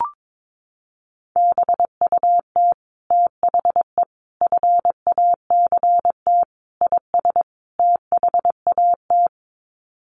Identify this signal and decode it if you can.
Morse code, heard as a keyed tone, about 22 wpm, 700 Hz. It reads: BUT THE FACT IS THAT
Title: Sets of 5 Words Encore2 22wpm